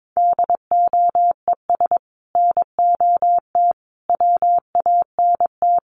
Morse code, heard as a keyed tone, about 22 wpm, 700 Hz. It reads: DOES NOT WANT
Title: Sets of 4 Words 22wpm